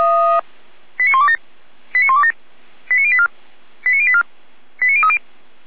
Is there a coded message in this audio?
Yes: five-tone selective call (ZVEI) from 1.0 to 5.2 s, heard as quick runs of five tones, tones of about 70 ms. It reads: 89127 89127 89084 89084 89030